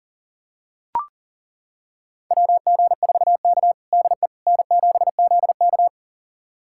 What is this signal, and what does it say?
Morse code, heard as a keyed tone, about 37 wpm, 700 Hz. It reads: WG4K DE N7ZK
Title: USA Callsign DE Callsign - 4 Characters - 40wpm